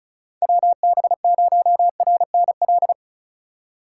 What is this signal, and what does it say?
Morse code, heard as a keyed tone, about 36 wpm, 700 Hz. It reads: WB0RNL
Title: USA Call Signs - 6 Characters - 3x Character Spacing - 35wpm